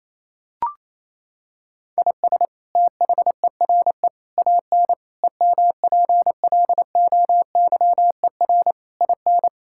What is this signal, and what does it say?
Morse code, heard as a keyed tone, about 28 wpm, 700 Hz. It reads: IS THERE AN EMPLOYER IN
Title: Sentences from Top 1600 Words 28wpm